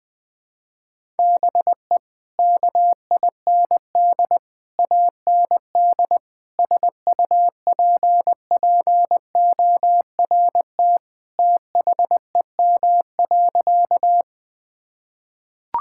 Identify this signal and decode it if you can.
Morse code, heard as a keyed tone, about 21 wpm, 700 Hz. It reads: BE KIND AND SUPPORT THEM.
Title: Sentences from Top 1000 Words - Review Part 5 - 20wpm